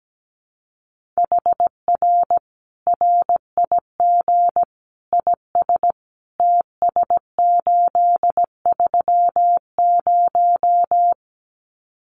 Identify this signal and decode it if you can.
Morse code, heard as a keyed tone, about 17 wpm, 700 Hz. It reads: HR RIG IS TS830
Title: Common QSO Phrases 17wpm